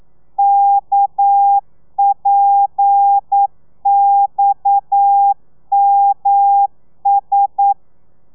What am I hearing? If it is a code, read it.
Morse code, heard as a keyed tone, about 9 wpm, 800 Hz. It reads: KPXMS
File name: telegrafia / i2tfj / Lez7 40